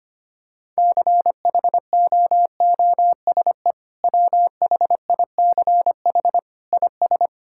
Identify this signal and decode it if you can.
Morse code, heard as a keyed tone, about 25 wpm, 700 Hz. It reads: CHOOSE WHICH IS
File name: practice / 25wpm / Sentences with 2-Word Phrases - 25wpm